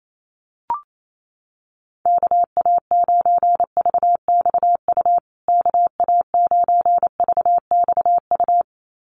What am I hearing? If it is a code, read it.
Morse code, heard as a keyed tone, about 28 wpm, 700 Hz. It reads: KA9VXU KA9VXU
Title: USA Call Signs Repeated 2x 28wpm